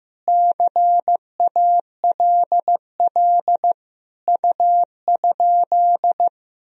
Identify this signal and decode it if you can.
Morse code, heard as a keyed tone, about 15 wpm, 700 Hz. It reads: CALL U?
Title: Sentences from Top 100 Words and Abbreviations 15wpm